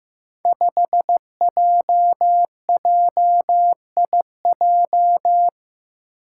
Morse code, heard as a keyed tone, about 15 wpm, 700 Hz. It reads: 5JJIJ
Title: Five Challenging Letter-Number Combinations 15wpm